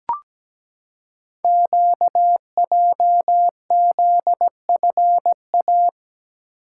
Morse code, heard as a keyed tone, about 17 wpm, 700 Hz. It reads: QJZFA
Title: Code Groups - 5 Letters - 17wpm